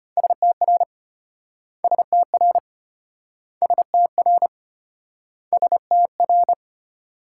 Morse code, heard as a keyed tone, about 33 wpm, 700 Hz. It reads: STR STR STR STR